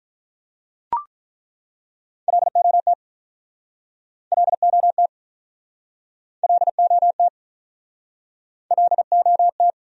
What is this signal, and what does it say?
Morse code, heard as a keyed tone, about 50 wpm, 700 Hz. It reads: LOT LOT LOT LOT